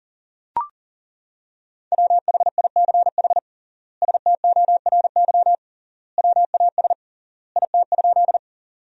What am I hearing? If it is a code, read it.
Morse code, heard as a keyed tone, about 40 wpm, 700 Hz. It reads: WHICH STORY WAS IT?
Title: 3x Word-Spacing - Sentences from Top 300 Words - 40wpm